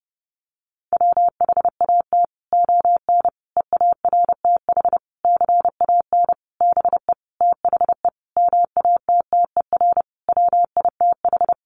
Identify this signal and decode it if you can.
Morse code, heard as a keyed tone, about 30 wpm, 700 Hz. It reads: WHAT ON EARTH CAN BE THE MATTER WITH
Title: Sentences from Top 700 Words 30wpm